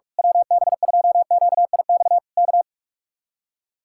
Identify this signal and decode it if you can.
Morse code, heard as a keyed tone, about 41 wpm, 700 Hz. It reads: WB2QIX K
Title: Speed-Racing Common QSO Phrases - 45wpm